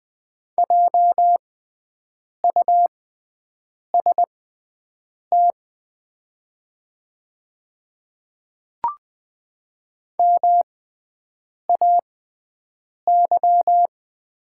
Morse code, heard as a keyed tone, about 20 wpm, spaced as for 7 wpm, 700 Hz. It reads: JUST MAY
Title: 6x Character Spacing - Top 200 Words - 20wpm